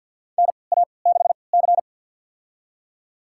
Morse code, heard as a keyed tone, about 47 wpm, spaced as for 28 wpm, 700 Hz. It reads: NA6C
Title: USA Call Signs - 4 Characters - 3x Character Spacing - 50wpm